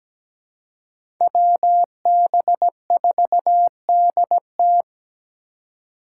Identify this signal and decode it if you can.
Morse code, heard as a keyed tone, about 17 wpm, 700 Hz. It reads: WB4DT
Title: USA Call Signs - 5 Characters - 2x Character Spacing - 17wpm